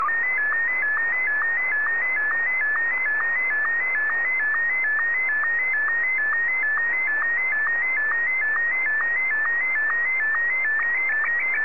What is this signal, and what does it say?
SSTV (slow-scan television), Martin 1.